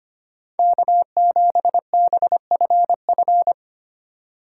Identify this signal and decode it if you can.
Morse code, heard as a keyed tone, about 25 wpm, 700 Hz. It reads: K7BFF
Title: USA Call Signs - 5 Characters - 4x Character Spacing - 25wpm